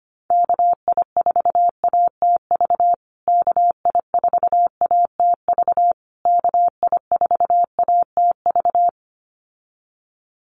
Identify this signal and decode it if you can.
Morse code, heard as a keyed tone, about 25 wpm, 700 Hz. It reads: KI4ATV KI4ATV KI4ATV
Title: USA Call Signs Repeated 3x 25wpm